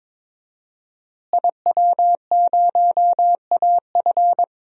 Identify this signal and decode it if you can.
Morse code, heard as a keyed tone, about 22 wpm, 700 Hz. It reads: IW0AF
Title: DXCC-100 Call Signs - 5 Characters - Repeated 2x - 22wpm